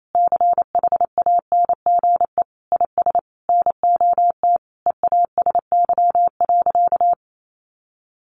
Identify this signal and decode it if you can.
Morse code, heard as a keyed tone, about 28 wpm, 700 Hz. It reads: CHANGE IS NOT EASY.
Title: Sentences from Top 1500 Words 28wpm